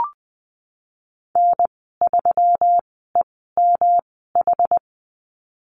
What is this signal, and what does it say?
Morse code, heard as a keyed tone, about 20 wpm, spaced as for 14 wpm, 700 Hz. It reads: N3EMH
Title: 122 Lesson 20g - Number 3 - Call Signs 20wpm